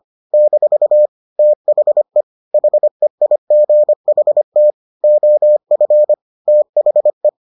Morse code, heard as a keyed tone, about 25 wpm, 600 Hz. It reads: - THE HEIGHT OF THE